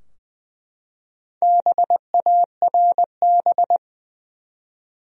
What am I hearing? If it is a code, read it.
Morse code, heard as a keyed tone, about 20 wpm, 700 Hz. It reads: BARB
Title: USA Common Names - Max 4 Letters - 20wpm